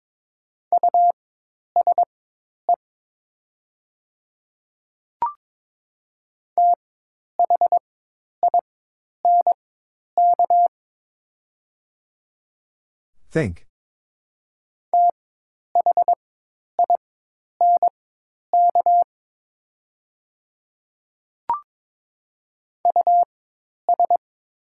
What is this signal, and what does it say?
Morse code, heard as a keyed tone, about 22 wpm, spaced as for 10 wpm, 700 Hz. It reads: USE THINK THINK US
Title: 4x Character Spacing - Top 200 Words - 22wpm